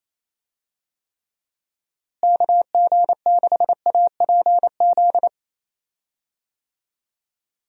Morse code, heard as a keyed tone, about 28 wpm, 700 Hz. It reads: KG6APZ
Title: USA Call Signs - 6 Characters - Repeated 3x - 28wpm